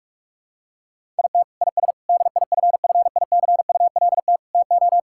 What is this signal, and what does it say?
Morse code, heard as a keyed tone, about 46 wpm, 700 Hz. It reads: IT IS DIFFICULT TO
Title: Sentences from Top 1000 Words - Review Part 5 - 45wpm